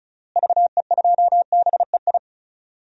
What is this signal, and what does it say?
Morse code, heard as a keyed tone, about 34 wpm, 700 Hz. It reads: VE2BEI
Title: DXCC-100 Call Signs - 6 Characters - Repeated 3x - 35wpm